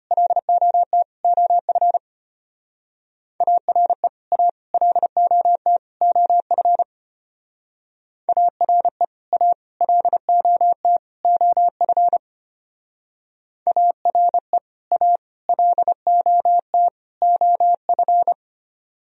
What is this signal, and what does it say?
Morse code, heard as a keyed tone, about 31 wpm, 700 Hz. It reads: LOT OF ARE A LOT OF ARE A LOT OF ARE A LOT OF
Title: Speed-Racing - Sets of 4 Words 25wpm